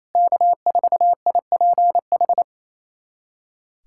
Morse code, heard as a keyed tone, about 28 wpm, 700 Hz. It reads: K4IPH
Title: USA Callsign DE Callsign 28wpm